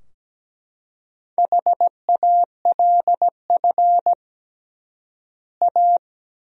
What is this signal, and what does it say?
Morse code, heard as a keyed tone, about 17 wpm, 700 Hz. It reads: HALF A
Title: 3x Word-Spacing - Sentences from Top 400 Words - 17wpm